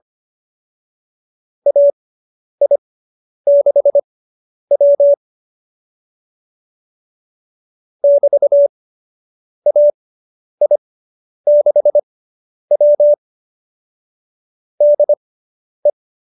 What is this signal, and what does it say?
Morse code, heard as a keyed tone, about 25 wpm, spaced as for 10 wpm, 590 Hz. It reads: AI6W <BT> AI6W DE